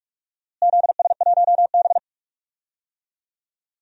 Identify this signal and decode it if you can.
Morse code, heard as a keyed tone, about 46 wpm, 700 Hz. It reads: ZS1B
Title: DXCC-100 Call Signs - 4 Characters - Repeated 2x - 45wpm